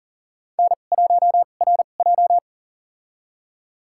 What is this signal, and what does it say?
Morse code, heard as a keyed tone, about 38 wpm, spaced as for 28 wpm, 700 Hz. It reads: N1RJ
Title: USA Call Signs - 4 Characters - 2x Character Spacing - 40wpm